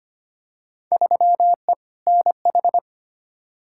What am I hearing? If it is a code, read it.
Morse code, heard as a keyed tone, about 25 wpm, 700 Hz. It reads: 3E NH
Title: Field Day Exchanges Encore - 25wpm